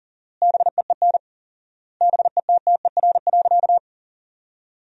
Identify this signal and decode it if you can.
Morse code, heard as a keyed tone, about 40 wpm, 700 Hz. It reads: BEEN BETTER.
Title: 4x Word-Spacing - Sentences from Top 500 Words - 40wpm